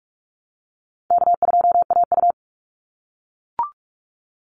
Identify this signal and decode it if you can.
Morse code, heard as a keyed tone, about 44 wpm, 700 Hz. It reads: K2AU
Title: Call Signs - 45wpm